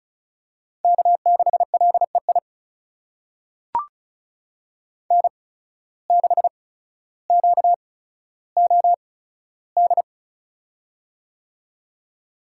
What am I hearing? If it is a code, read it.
Morse code, heard as a keyed tone, about 36 wpm, spaced as for 15 wpm, 700 Hz. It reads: K6LEI N6QOD
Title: USA Call Signs - 5 Characters - 8x Character Spacing - 35wpm